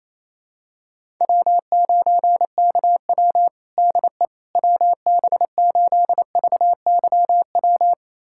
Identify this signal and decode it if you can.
Morse code, heard as a keyed tone, about 28 wpm, 700 Hz. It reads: W9KW DE WB8VYW
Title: USA Callsign DE Callsign 28wpm